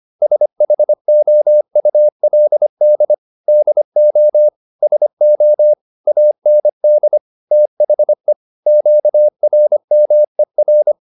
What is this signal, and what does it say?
Morse code, heard as a keyed tone, about 25 wpm, 600 Hz. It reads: SHOULD DO SO AND THE QRMER